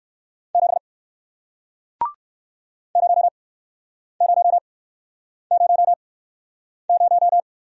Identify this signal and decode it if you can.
Morse code, heard as a keyed tone, about 61 wpm, spaced as for 9 wpm, 700 Hz. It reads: 7 0000